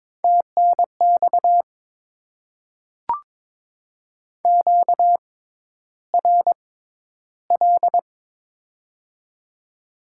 Morse code, heard as a keyed tone, about 22 wpm, spaced as for 11 wpm, 700 Hz. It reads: TNX QRL
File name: Top 20 QSO Elements - 6x Character Spacing - 22wpm